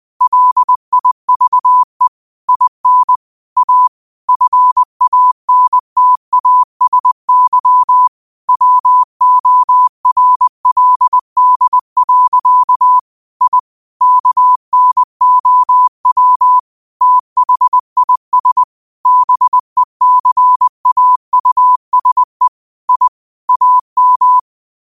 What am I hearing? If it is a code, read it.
Morse code, heard as a keyed tone, about 20 wpm, 1000 Hz. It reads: LIVE IN A FANTASY WORLD. I KNOW THIS BECAUSE I AM
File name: qotdfiles / 20wpm